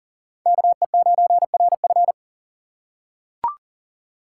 Morse code, heard as a keyed tone, about 40 wpm, 700 Hz. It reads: KE9RF
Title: USA Call Signs - 6x Character Spacing - 40wpm